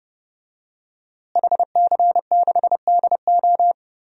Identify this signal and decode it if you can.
Morse code, heard as a keyed tone, about 30 wpm, 700 Hz. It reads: HC6DO
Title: DXCC-100 Call Signs - 5 Characters - Repeated 3x - 30wpm